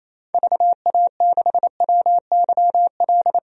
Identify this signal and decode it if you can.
Morse code, heard as a keyed tone, about 28 wpm, 700 Hz. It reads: VA6WYL
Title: DXCC-100 Call Signs - 6 Characters - 28wpm